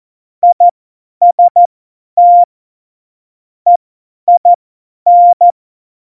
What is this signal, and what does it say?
Morse code, heard as a keyed tone, about 14 wpm, spaced as for 10 wpm, 700 Hz. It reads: IST EIN
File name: Morsekurs / Witze / Witz 18 Juli 17, Tempo 50 F70